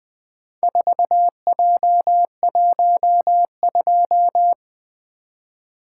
Morse code, heard as a keyed tone, about 20 wpm, 700 Hz. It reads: 4J12
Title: Four Letter-Number Combinations - 20wpm